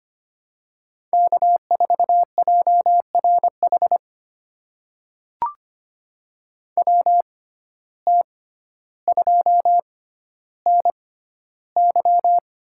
Morse code, heard as a keyed tone, about 25 wpm, spaced as for 13 wpm, 700 Hz. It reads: K4JRH WT2NY
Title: USA Call Signs - 5 Characters - 6x Character Spacing - 25wpm